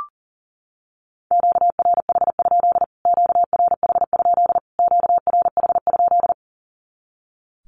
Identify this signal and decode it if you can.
Morse code, heard as a keyed tone, about 40 wpm, 700 Hz. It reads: QRH? QRH? QRH?
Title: Q-Codes Repeated 3x 40wpm